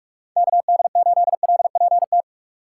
Morse code, heard as a keyed tone, about 42 wpm, 700 Hz. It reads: KD8LPT
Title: USA Call Signs - 8x Character Spacing - 45wpm